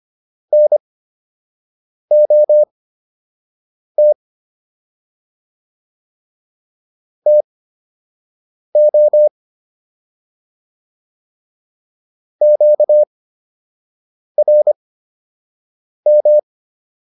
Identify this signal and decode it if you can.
Morse code, heard as a keyed tone, about 25 wpm, spaced as for 6 wpm, 600 Hz. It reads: NOT TO QRM